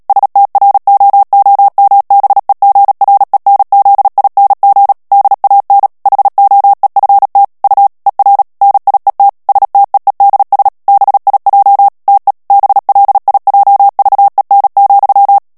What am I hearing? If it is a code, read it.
Morse code, heard as a keyed tone, about 37 wpm, 800 Hz. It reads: STROOMBEGRENZING DAN HOEFT U ER NIET STEEDS BIJ TE BLIJVEN,
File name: morse mp3 / HOLLANDAIS texte clair 2 vitesse 35 mots minute